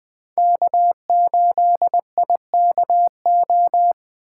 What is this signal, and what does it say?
Morse code, heard as a keyed tone, about 20 wpm, 700 Hz. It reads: K8IKO
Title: 2023 POTA Hunter Call Signs - 20wpm